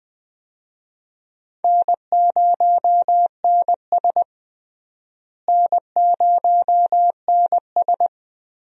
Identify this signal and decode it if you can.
Morse code, heard as a keyed tone, about 20 wpm, 700 Hz. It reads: N0NS N0NS
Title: USA Call Signs - 4 Characters - Repeated 2x - 20wpm